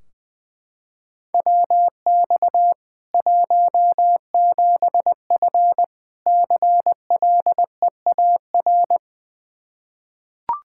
Morse code, heard as a keyed tone, about 20 wpm, 700 Hz. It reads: WX 17F CLEAR
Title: Practice QSO - Exchange 2 - Info - 20wpm